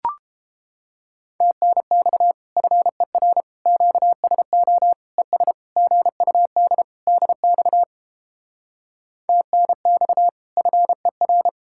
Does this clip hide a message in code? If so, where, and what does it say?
Yes: Morse code from 1.4 to 11.5 s, heard as a keyed tone, about 31 wpm, 700 Hz. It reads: TNX FER QSO ES GUD DX TNX FER